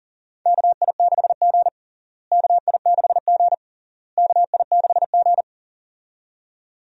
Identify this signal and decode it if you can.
Morse code, heard as a keyed tone, about 40 wpm, 700 Hz. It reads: KI6G KI6G KI6G
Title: USA Call Signs - 4 Characters - Repeated 3x - 40wpm